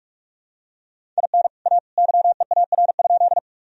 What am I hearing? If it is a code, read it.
Morse code, heard as a keyed tone, about 41 wpm, 700 Hz. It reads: IN A YEAR?
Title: Sentences from Top 1000 Words - Review Part 1 - 45wpm